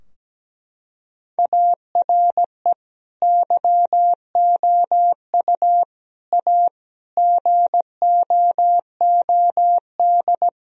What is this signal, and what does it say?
Morse code, heard as a keyed tone, about 17 wpm, 700 Hz. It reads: ARE YOU A GOOD
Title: Sentences from Top 1000 Words - Review Part 4 - 17wpm